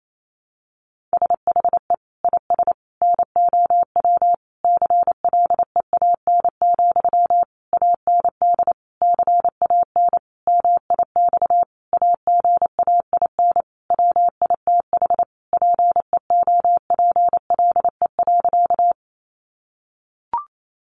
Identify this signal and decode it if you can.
Morse code, heard as a keyed tone, about 28 wpm, 700 Hz. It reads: SHE IS NOW CLEAN, AND CAN MIX AGAIN WITH PEOPLE.
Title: Sentences from Top 1100 Words 28wpm